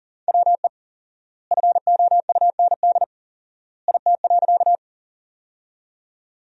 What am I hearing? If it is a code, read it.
Morse code, heard as a keyed tone, about 40 wpm, 700 Hz. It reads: WE FOUND IT.
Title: 4x Word-Spacing - Sentences from Top 300 Words - 40wpm